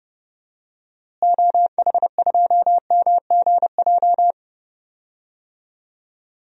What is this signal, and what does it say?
Morse code, heard as a keyed tone, about 30 wpm, 700 Hz. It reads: OH2MGJ
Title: DXCC-100 Call Signs - 6 Characters - Repeated 3x - 30wpm